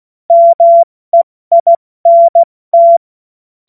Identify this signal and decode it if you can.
Morse code, heard as a keyed tone, about 16 wpm, 680 Hz. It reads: MEINT
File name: Morsekurs / Witze / Witz 01. März 2023 70 F80